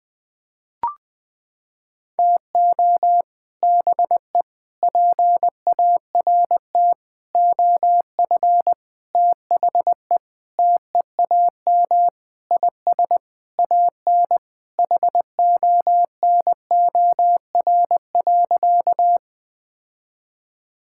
Morse code, heard as a keyed tone, about 21 wpm, 700 Hz. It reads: TO BE PART OF THE TEAM IS AN HONOR.
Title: Sentences with 5-Word Phrases - 20wpm